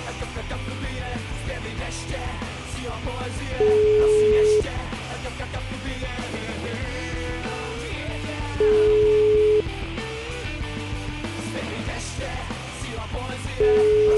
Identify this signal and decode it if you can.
Ringback tone (Europe, 425 Hz), with noise.